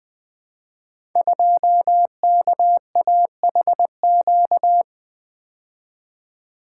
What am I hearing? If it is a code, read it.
Morse code, heard as a keyed tone, about 20 wpm, 700 Hz. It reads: 2KAHQ
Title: Code Groups - 5 Letter:Numbers - 20wpm